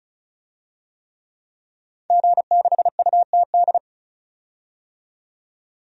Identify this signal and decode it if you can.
Morse code, heard as a keyed tone, about 34 wpm, 700 Hz. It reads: G6UTD
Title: DXCC-100 Call Signs - 5 Characters - 35wpm